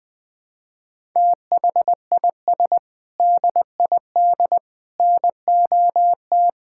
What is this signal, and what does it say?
Morse code, heard as a keyed tone, about 21 wpm, 700 Hz. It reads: THIS DID NOT
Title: Sentences from Top 1000 Words - Review Part 6 - 20wpm